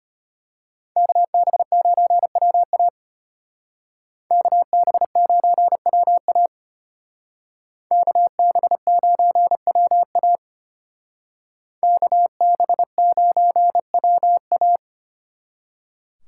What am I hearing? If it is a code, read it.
Morse code, heard as a keyed tone, about 32 wpm, 700 Hz. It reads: KB9WA KB9WA KB9WA KB9WA